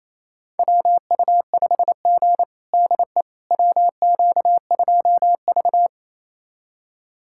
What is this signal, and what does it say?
Morse code, heard as a keyed tone, about 28 wpm, 700 Hz. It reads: WU5G DE WQ2V
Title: USA Callsign DE Callsign - 4 Characters - 28wpm